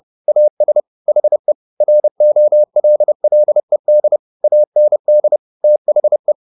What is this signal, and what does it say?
Morse code, heard as a keyed tone, about 30 wpm, 600 Hz. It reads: AS HE ROLLED AND THE